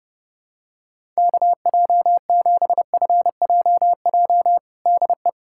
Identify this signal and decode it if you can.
Morse code, heard as a keyed tone, about 28 wpm, 700 Hz. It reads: KJ7FJJ DE
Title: USA Callsign DE Callsign - 6 Characters - 30wpm